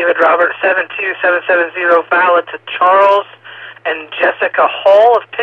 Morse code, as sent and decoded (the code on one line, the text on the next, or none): none